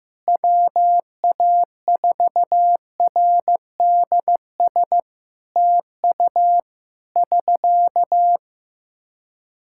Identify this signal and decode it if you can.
Morse code, heard as a keyed tone, about 15 wpm, 700 Hz. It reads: WA4RDS TU <SK>
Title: Practice QSO - Exchange 4 - Ending - 15wpm